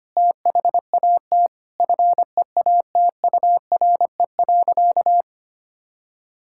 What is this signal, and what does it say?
Morse code, heard as a keyed tone, about 25 wpm, 700 Hz. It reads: THAT FEATURE.
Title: Sentences from Top 1000 Words - Review Part 5 - 25wpm